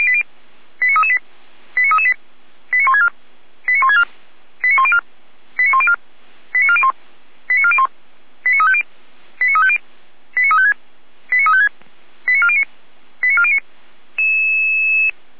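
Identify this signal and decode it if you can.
Five-tone selective call (ZVEI), heard as quick runs of five tones, tones of about 70 ms. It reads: 89338 89338 89164 89164 89104 89104 89401 89401 89350 89350 89356 89356 89409 89409
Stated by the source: ZVEI five-tone call 89701 received by SDR.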